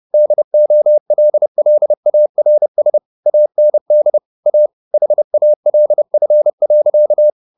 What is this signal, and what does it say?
Morse code, heard as a keyed tone, about 30 wpm, 600 Hz. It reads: DOLLARS AND A HALF.
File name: the old man and the sea / the old man and the sea 1 30wpm